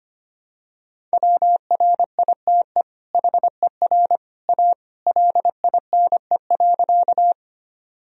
Morse code, heard as a keyed tone, about 25 wpm, 700 Hz. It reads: WRITE HER A LINE.
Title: Sentences from Top 1000 Words - Review Part 1 - 25wpm